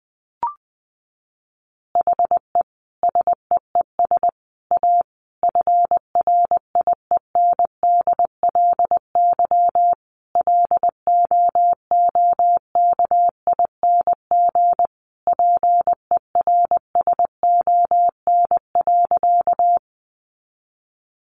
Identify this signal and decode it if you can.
Morse code, heard as a keyed tone, about 20 wpm, 700 Hz. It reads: HE SEES A FRIENDLY LOOKING PERSON.
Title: Easy Phrases 20wpm